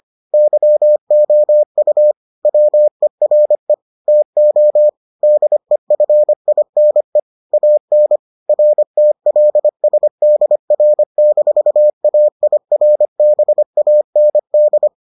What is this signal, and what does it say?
Morse code, heard as a keyed tone, about 25 wpm, 600 Hz. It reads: YOU WERE TO DEFINE AN RTLSDR-AIRBAND